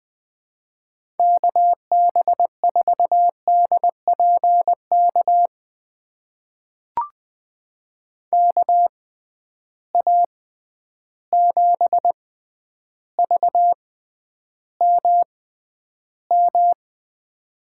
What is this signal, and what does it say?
Morse code, heard as a keyed tone, about 20 wpm, spaced as for 10 wpm, 700 Hz. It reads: KB4DPK KA7VMM
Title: USA Call Signs - 6x Character Spacing - 20wpm